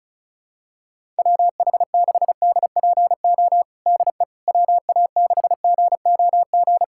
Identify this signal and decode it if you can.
Morse code, heard as a keyed tone, about 34 wpm, 700 Hz. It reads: WH6DPO DE WA6GOG
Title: USA Callsign DE Callsign 35wpm